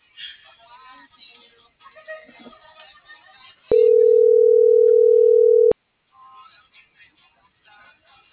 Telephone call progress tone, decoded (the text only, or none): ringback tone